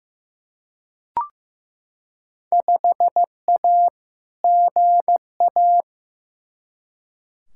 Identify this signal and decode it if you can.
Morse code, heard as a keyed tone, about 15 wpm, 700 Hz. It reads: 5A GA
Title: Field Day Exchanges Encore - 15wpm